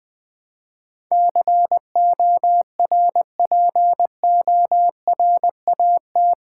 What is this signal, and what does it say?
Morse code, heard as a keyed tone, about 20 wpm, 700 Hz. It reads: CORPORAT
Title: Build A Word - 4 Letters At A Time - 11-Letter Words - 20wpm